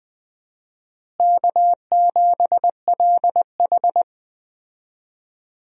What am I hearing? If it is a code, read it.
Morse code, heard as a keyed tone, about 20 wpm, 700 Hz. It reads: K7LH
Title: USA Call Signs - 4 Characters - Repeated 2x - 20wpm